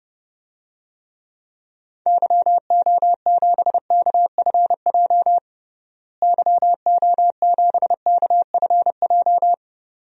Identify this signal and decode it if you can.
Morse code, heard as a keyed tone, about 30 wpm, 700 Hz. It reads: YO7KFJ YO7KFJ
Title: DXCC-100 Call Signs - 6 Characters - Repeated 2x - 30wpm